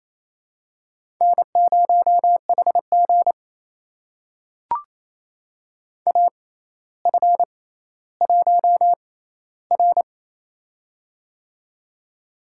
Morse code, heard as a keyed tone, about 28 wpm, spaced as for 14 wpm, 700 Hz. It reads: N0HG AF1R
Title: USA Call Signs - 4 Characters - 6x Character Spacing - 28wpm